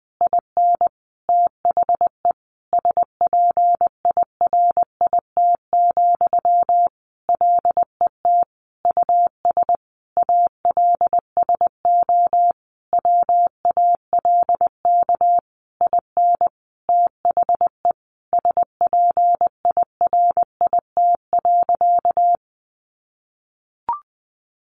Morse code, heard as a keyed tone, about 20 wpm, 700 Hz. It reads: IN THE SPIRIT, LET US ALSO WALK IN THE SPIRIT.
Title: Sentences from Top 1400 Words 20wpm